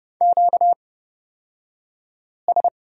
Morse code, heard as a keyed tone, about 30 wpm, spaced as for 5 wpm, 700 Hz. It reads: QS
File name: Lesson 38d - Letter q - Two Characters 30wpm